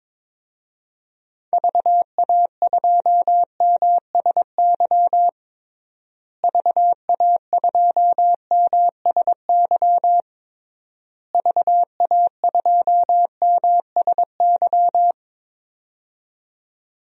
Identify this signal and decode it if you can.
Morse code, heard as a keyed tone, about 22 wpm, 700 Hz. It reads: VA2MSY VA2MSY VA2MSY
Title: DXCC-100 Call Signs - 6 Characters - Repeated 3x - 22wpm